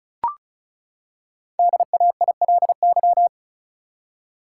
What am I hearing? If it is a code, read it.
Morse code, heard as a keyed tone, about 36 wpm, 700 Hz. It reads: DAILY
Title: Five Letter Words - 35wpm